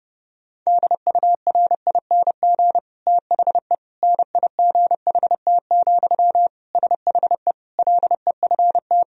Morse code, heard as a keyed tone, about 32 wpm, 700 Hz. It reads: DURING THE NIGHT, SHE LEFT
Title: Sentences from Top 1000 Words - Review Part 2 - 30wpm